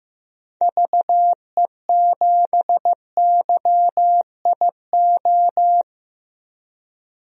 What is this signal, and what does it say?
Morse code, heard as a keyed tone, about 15 wpm, 700 Hz. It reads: VE7YIO
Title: DXCC-100 Call Signs - 6 Characters - 15wpm